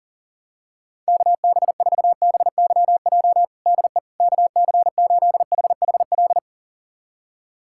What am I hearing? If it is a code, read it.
Morse code, heard as a keyed tone, about 37 wpm, 700 Hz. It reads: KB4BYJ DE KC8HHL
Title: USA Callsign DE Callsign 40wpm